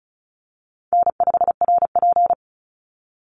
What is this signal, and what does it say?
Morse code, heard as a keyed tone, about 34 wpm, 700 Hz. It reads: N5RP
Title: Call Signs Repeated 3x - 35wpm